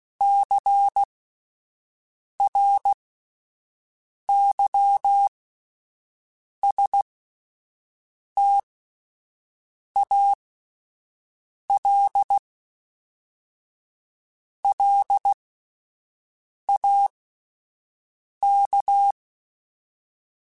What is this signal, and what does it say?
Morse code, heard as a keyed tone, about 16 wpm, spaced as for 5 wpm, 800 Hz. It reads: CRYSTALLAK
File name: fromSMs / FromSMs mp3s / Cities@5